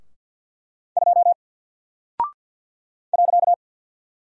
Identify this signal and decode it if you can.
Morse code, heard as a keyed tone, about 48 wpm, 700 Hz. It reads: 2 .
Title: Single Character (Letter:Number:Punctuation:Prosign) Alt 50wpm